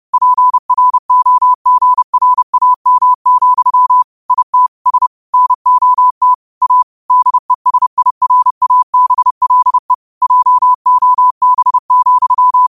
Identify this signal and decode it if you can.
Morse code, heard as a keyed tone, about 30 wpm, 1000 Hz. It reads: PROGRAM, IT S NOT A DESIRABLE JOB,
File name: qotdfiles / 30wpm